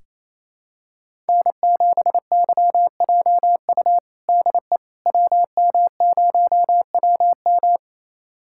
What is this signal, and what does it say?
Morse code, heard as a keyed tone, about 28 wpm, 700 Hz. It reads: N7YJU DE WM0WM
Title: USA Callsign DE Callsign - 5 Characters - 28wpm